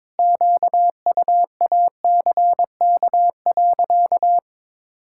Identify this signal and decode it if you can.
Morse code, heard as a keyed tone, about 22 wpm, 700 Hz. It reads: QUACK.
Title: Pangram Challenge v2 - 22wpm